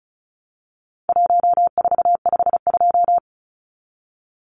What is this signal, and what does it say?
Morse code, heard as a keyed tone, about 34 wpm, 700 Hz. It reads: 1452
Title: Numbers - Contest Style - 4-Digit 35wpm